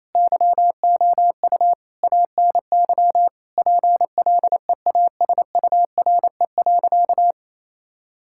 Morse code, heard as a keyed tone, about 28 wpm, 700 Hz. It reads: YOU ANY PLEASURE.
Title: Sentences from Top 1600 Words 28wpm